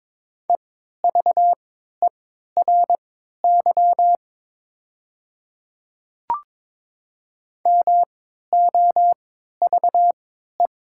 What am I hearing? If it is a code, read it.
Morse code, heard as a keyed tone, about 22 wpm, spaced as for 12 wpm, 700 Hz. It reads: EVERY MOVE
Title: 3x Character Spacing - Top 500 Words - 22wpm